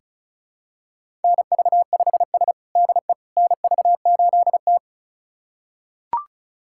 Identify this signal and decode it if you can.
Morse code, heard as a keyed tone, about 34 wpm, 700 Hz. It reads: NV5S DE NV8T
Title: USA Callsign DE Callsign - 4 Characters - 35wpm